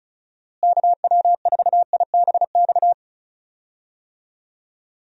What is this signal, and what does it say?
Morse code, heard as a keyed tone, about 36 wpm, 700 Hz. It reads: KW4IBX
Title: USA Call Signs - 6 Characters - Repeated 3x - 35wpm